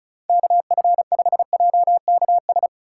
Morse code, heard as a keyed tone, about 36 wpm, 700 Hz. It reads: KF5JKS
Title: USA Call Signs - 6 Characters - 8x Character Spacing - 35wpm